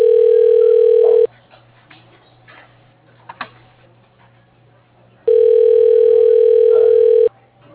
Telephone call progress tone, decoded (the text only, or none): ringback tone